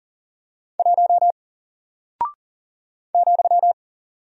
Morse code, heard as a keyed tone, about 40 wpm, 700 Hz. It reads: 1 ,
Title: Single Character (Letter:Number:Punctuation:Prosign) 40wpm